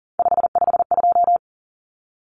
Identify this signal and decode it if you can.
Morse code, heard as a keyed tone, about 40 wpm, 700 Hz. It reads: 552
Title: Numbers - 3-Digit 40wpm